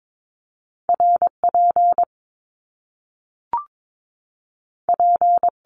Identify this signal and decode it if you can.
Morse code, heard as a keyed tone, about 22 wpm, 700 Hz. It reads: RP P
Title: Top Two Letter Combinations in Words 22wpm